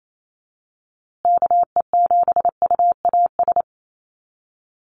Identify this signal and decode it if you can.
Morse code, heard as a keyed tone, about 28 wpm, 700 Hz. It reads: KE7UAS
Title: USA Call Signs Repeated 3x 28wpm